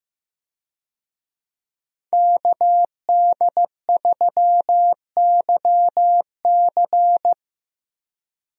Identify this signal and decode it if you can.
Morse code, heard as a keyed tone, about 15 wpm, 700 Hz. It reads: KD3YC
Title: USA Call Signs - 5 Characters - Repeated 3x - 15wpm